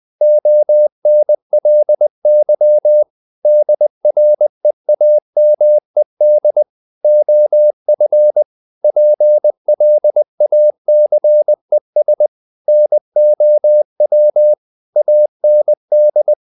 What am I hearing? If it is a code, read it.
Morse code, heard as a keyed tone, about 20 wpm, 600 Hz. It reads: ONLY DREAMED OF PLACES NOW AND
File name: the old man and the sea / the old man and the sea part 1 20wpm